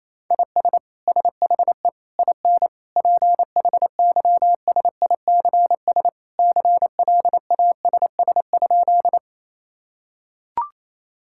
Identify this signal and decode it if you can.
Morse code, heard as a keyed tone, about 28 wpm, 700 Hz. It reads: IS SHE IN PHYSICS CLASS?
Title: Sentences from Top 1000 Words - Review Part 5 - 28wpm